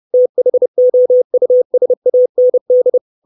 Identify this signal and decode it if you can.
Morse code, heard as a keyed tone, about 30 wpm, 500 Hz. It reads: THOUSAND
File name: 632 Words 30WPM